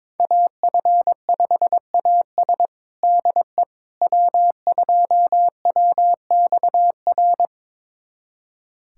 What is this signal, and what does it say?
Morse code, heard as a keyed tone, about 22 wpm, 700 Hz. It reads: AF5AS DE W2WXR
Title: USA Callsign DE Callsign - 5 Characters - 22wpm